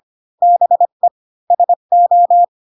Morse code, heard as a keyed tone, about 25 wpm, 700 Hz. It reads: BE SO
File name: Pinocchio / Pinocchio ch5 20 25 ik1hge